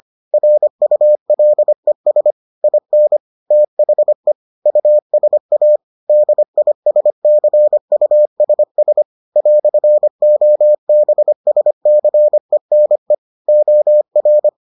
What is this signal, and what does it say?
Morse code, heard as a keyed tone, about 25 wpm, 600 Hz. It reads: RULES IN THE USA DISCUSS "OBSCENE OR